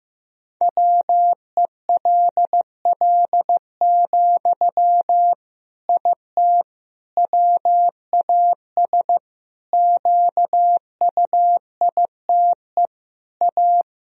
Morse code, heard as a keyed tone, about 15 wpm, 700 Hz. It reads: WELL, IT WAS QUITE A
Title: Sentences from Top 1000 Words - Review Part 3 - 15wpm